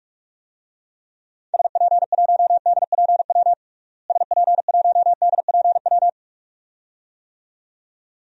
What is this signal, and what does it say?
Morse code, heard as a keyed tone, about 46 wpm, 700 Hz. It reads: SP1DPW SP1DPW
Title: DXCC-100 Call Signs - 6 Characters - Repeated 2x - 45wpm